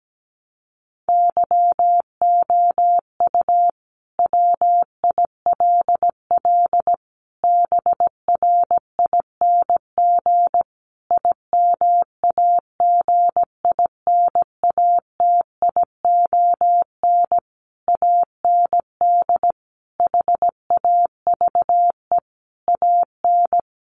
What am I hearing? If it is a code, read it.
Morse code, heard as a keyed tone, about 17 wpm, 700 Hz. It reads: YOU WILL BRING IMAGINATION AND HAVE AN
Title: Sentences from Top 1300 Words 17wpm